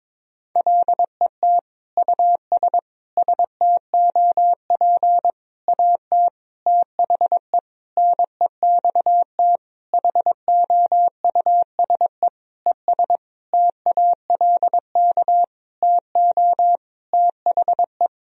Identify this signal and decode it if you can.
Morse code, heard as a keyed tone, about 22 wpm, 700 Hz. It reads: LET US STOP AT THE NEXT HOUSE ES TALK TO THE
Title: Sentences from Top 400 Words and Abbreviations 22wpm